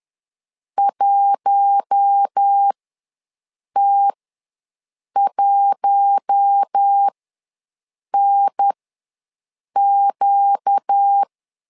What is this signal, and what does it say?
Morse code, heard as a keyed tone, about 10 wpm, spaced as for 6 wpm, 800 Hz. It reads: 1T1NQ
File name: LEZIONI / 4 Q1 ws 10wpm 2538